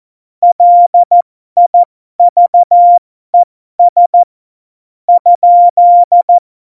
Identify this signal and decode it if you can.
Morse code, heard as a keyed tone, about 14 wpm, spaced as for 12 wpm, 700 Hz. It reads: LIVES ?
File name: Morsekurs / Witze